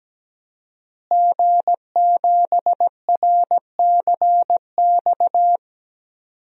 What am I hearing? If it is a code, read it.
Morse code, heard as a keyed tone, about 17 wpm, 700 Hz. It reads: G7RCX
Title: DXCC-100 Call Signs - 5 Characters - Repeated 3x - 17wpm